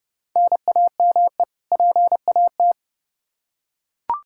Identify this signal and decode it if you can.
Morse code, heard as a keyed tone, about 30 wpm, 700 Hz. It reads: NAME PAT
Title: Practice QSO - Exchange 1 - Beginning - 30wpm